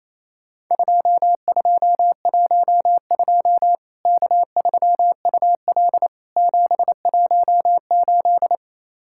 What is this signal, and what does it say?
Morse code, heard as a keyed tone, about 28 wpm, 700 Hz. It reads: 2212 K3UL 718
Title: Worked All Europe DX Contest - QTC Exchange - 28wpm